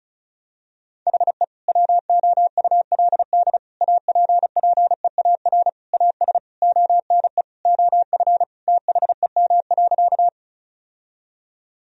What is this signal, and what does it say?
Morse code, heard as a keyed tone, about 34 wpm, 700 Hz. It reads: HE WOULD APPEAR AS ONE OF THEM.
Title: Sentences from Top 1000 Words - Review Part 2 - 35wpm